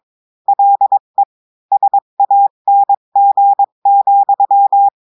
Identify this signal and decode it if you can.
Morse code, heard as a keyed tone, about 22 wpm, 800 Hz. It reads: LE SANG,
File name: cours-du-mercredi / audios / cours-853-du-2-mai-2024-etxte chiffre et texte clair-a-20-wpm